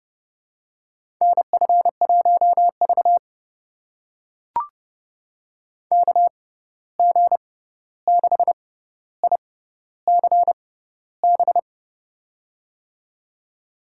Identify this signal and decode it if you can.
Morse code, heard as a keyed tone, about 31 wpm, spaced as for 14 wpm, 700 Hz. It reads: NF1V KG6ICB
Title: USA Call Signs - 6x Character Spacing - 30wpm